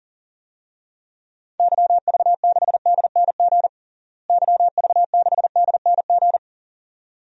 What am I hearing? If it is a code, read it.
Morse code, heard as a keyed tone, about 40 wpm, 700 Hz. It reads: YV6DNG YV6DNG
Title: DXCC-100 Call Signs - 6 Characters - Repeated 2x - 40wpm